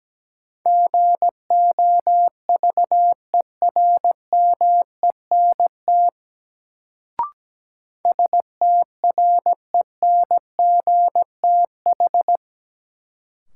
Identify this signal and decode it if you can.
Morse code, heard as a keyed tone, about 17 wpm, 700 Hz. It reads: GOVERMENT STRENGTH
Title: Sentences from Top 1300 Words - New Word Drill Alt - 17wpm